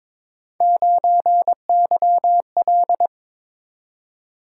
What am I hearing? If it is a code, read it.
Morse code, heard as a keyed tone, about 22 wpm, 700 Hz. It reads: 9YL
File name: practice / 22wpm / Three Letter-Number Combinations - 22wpm